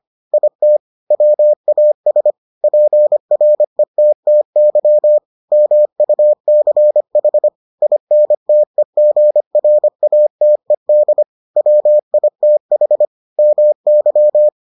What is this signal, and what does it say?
Morse code, heard as a keyed tone, about 25 wpm, 600 Hz. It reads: IT WAS PRETTY MUCH INTEGRATED WITH MY